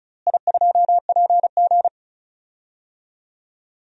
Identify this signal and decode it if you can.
Morse code, heard as a keyed tone, about 34 wpm, 700 Hz. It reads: I2PG
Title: DXCC-100 Call Signs - 4 Characters - Repeated 3x - 35wpm